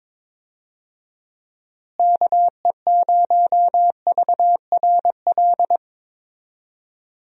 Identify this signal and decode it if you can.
Morse code, heard as a keyed tone, about 22 wpm, 700 Hz. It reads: KE0VRL
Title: USA Call Signs - 6 Characters - Repeated 3x - 22wpm